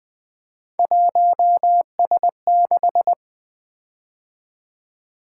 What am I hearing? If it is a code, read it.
Morse code, heard as a keyed tone, about 20 wpm, 700 Hz. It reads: 1S6
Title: Three Letter-Number Combinations - 20wpm